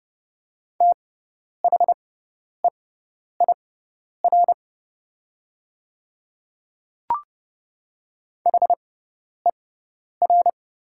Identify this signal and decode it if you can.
Morse code, heard as a keyed tone, about 30 wpm, spaced as for 10 wpm, 700 Hz. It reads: THEIR HER
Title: 6x Character Spacing - Top 100 Words - 30wpm